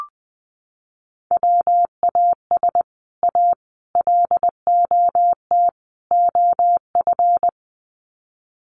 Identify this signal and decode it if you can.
Morse code, heard as a keyed tone, about 20 wpm, 700 Hz. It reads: WAS A LOT OF
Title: Sets of 4 Words 20wpm